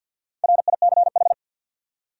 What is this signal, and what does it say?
Morse code, heard as a keyed tone, about 53 wpm, 700 Hz. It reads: RICH
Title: USA Common Names - Max 5 Letters - 50wpm